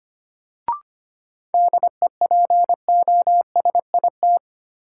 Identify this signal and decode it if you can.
Morse code, heard as a keyed tone, about 25 wpm, 700 Hz. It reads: DEPOSIT
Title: Sentences from Top 1300 Words - New Word Drill - 25wpm